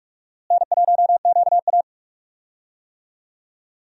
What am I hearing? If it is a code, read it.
Morse code, heard as a keyed tone, about 46 wpm, 700 Hz. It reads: N1QA
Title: USA Call Signs - 4 Characters - Repeated 2x - 45wpm